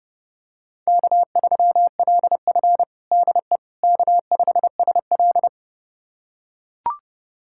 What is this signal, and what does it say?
Morse code, heard as a keyed tone, about 32 wpm, 700 Hz. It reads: K3LF DE K5SL
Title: USA Callsign DE Callsign - 4 Characters - 30wpm